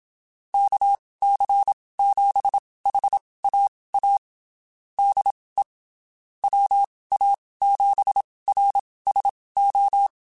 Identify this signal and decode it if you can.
Morse code, heard as a keyed tone, about 27 wpm, spaced as for 19 wpm, 800 Hz. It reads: KC7HAA DE WA7RSO
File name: (spaced) / QSOs Spaced-mp3s / kc7haa @25 sp